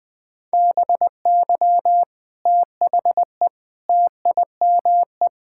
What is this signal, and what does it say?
Morse code, heard as a keyed tone, about 20 wpm, 700 Hz. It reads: BY THE TIME
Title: Build A Phrase - 3-Word Phrases - 20wpm